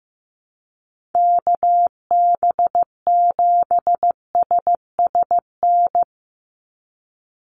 Morse code, heard as a keyed tone, about 15 wpm, 700 Hz. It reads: KB7SSN
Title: USA Call Signs 15wpm